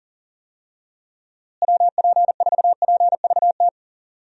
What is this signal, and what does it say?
Morse code, heard as a keyed tone, about 40 wpm, 700 Hz. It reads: WP4PVT
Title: USA Call Signs - 6 Characters - Repeated 2x - 40wpm